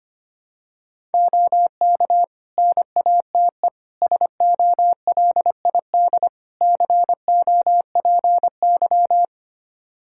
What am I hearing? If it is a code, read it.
Morse code, heard as a keyed tone, about 25 wpm, 700 Hz. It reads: OK NATE SOLID COPY
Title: Practice QSO - Exchange 3 - More Info - 25wpm